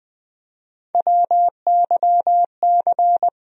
Morse code, heard as a keyed tone, about 20 wpm, 700 Hz. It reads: WYC
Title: USA Call Sign Postfixes - 3 Letters 20wpm